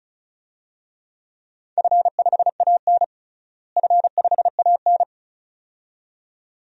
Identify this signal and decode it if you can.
Morse code, heard as a keyed tone, about 36 wpm, 700 Hz. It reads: F5AN F5AN
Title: DXCC-100 Call Signs - 4 Characters - Repeated 2x - 35wpm